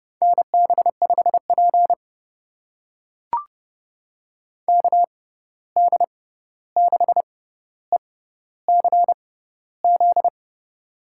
Morse code, heard as a keyed tone, about 30 wpm, spaced as for 14 wpm, 700 Hz. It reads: NB5P KD6ECZ
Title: USA Call Signs - 6x Character Spacing - 30wpm